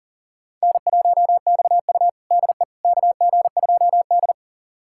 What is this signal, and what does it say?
Morse code, heard as a keyed tone, about 37 wpm, 700 Hz. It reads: N1XU DE KG2D
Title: USA Callsign DE Callsign - 4 Characters - 40wpm